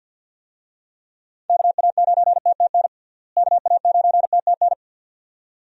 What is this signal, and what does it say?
Morse code, heard as a keyed tone, about 52 wpm, 700 Hz. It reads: KA9TTN KA9TTN
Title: USA Call Signs - 6 Characters - Repeated 2x - 50wpm